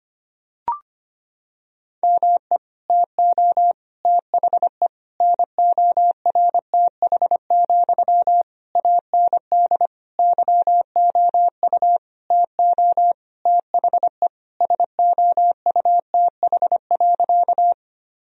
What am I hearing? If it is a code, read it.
Morse code, heard as a keyed tone, about 25 wpm, 700 Hz. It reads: ME TO THE NORTH, AND YOU TO THE SOUTH.
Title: Sentences from Top 1000 Words - Review Part 2 - 25wpm